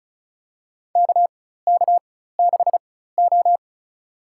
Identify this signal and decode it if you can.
Morse code, heard as a keyed tone, about 36 wpm, spaced as for 16 wpm, 700 Hz. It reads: KK6O
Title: USA Call Signs - 4 Characters - 4x Character Spacing - 35wpm